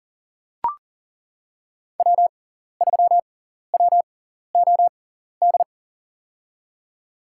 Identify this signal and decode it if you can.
Morse code, heard as a keyed tone, about 40 wpm, spaced as for 14 wpm, 700 Hz. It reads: W3WOD
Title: USA Call Signs - 6x Character Spacing - 40wpm